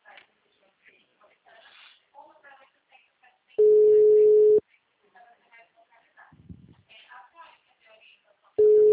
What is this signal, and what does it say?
Ringback tone (Europe, 425 Hz), with speech.
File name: out- -2036- - - .304